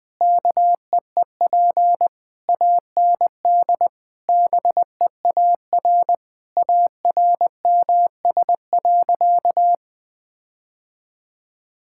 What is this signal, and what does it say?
Morse code, heard as a keyed tone, about 20 wpm, 700 Hz. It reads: KEEP AND BEAR ARMS.
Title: Sentences with 8-Word Phrases - 20wpm